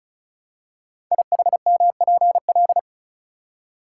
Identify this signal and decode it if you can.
Morse code, heard as a keyed tone, about 36 wpm, 700 Hz. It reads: IHMPL
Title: Code Groups - 5 Letters - 35wpm